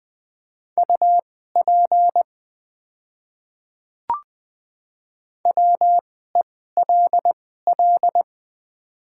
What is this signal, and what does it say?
Morse code, heard as a keyed tone, about 20 wpm, spaced as for 14 wpm, 690 Hz. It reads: UP WELL
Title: 2x Character Spacing - Top 200 Words - 20wpm